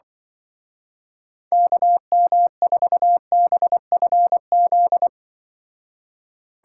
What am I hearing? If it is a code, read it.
Morse code, heard as a keyed tone, about 23 wpm, 700 Hz. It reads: KM4BFZ
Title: Speed Racing - USA Call Signs 20wpm